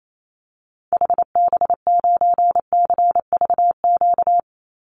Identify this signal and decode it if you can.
Morse code, heard as a keyed tone, about 28 wpm, 700 Hz. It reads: HB9CVQ
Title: Call Signs Repeated 2x - 28wpm